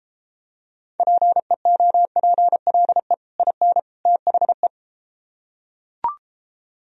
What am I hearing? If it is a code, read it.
Morse code, heard as a keyed tone, about 33 wpm, 700 Hz. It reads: PEOPLE IN THE